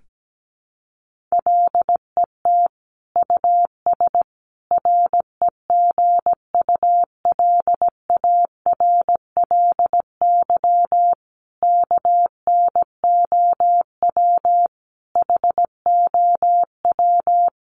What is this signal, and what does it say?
Morse code, heard as a keyed tone, about 17 wpm, 700 Hz. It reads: LET US REGULARLY KNOW HOW
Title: Sentences from Top 1400 Words 17wpm